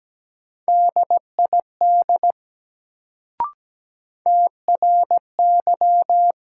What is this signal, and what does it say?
Morse code, heard as a keyed tone, about 17 wpm, 700 Hz. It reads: DID TRY
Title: Sets of 2 Words - New Word Drill - 17wpm